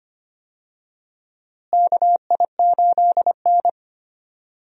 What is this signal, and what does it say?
Morse code, heard as a keyed tone, about 25 wpm, 700 Hz. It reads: KI8N
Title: USA Call Signs - 4 Characters - Repeated 2x - 25wpm